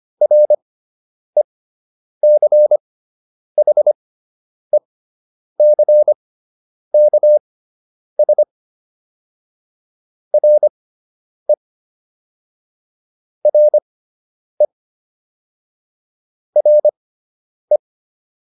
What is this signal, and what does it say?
Morse code, heard as a keyed tone, about 25 wpm, spaced as for 9 wpm, 600 Hz. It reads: RECHECKS RE RE RE